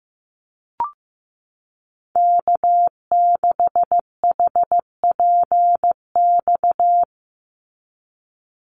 Morse code, heard as a keyed tone, about 15 wpm, 700 Hz. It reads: K6HPX
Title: NA SOTA Activator Callsigns 15wpm